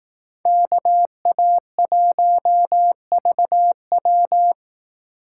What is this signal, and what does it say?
Morse code, heard as a keyed tone, about 18 wpm, 700 Hz. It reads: KA1VW
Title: Speed Racing - USA 2x2 Call Signs 15wpm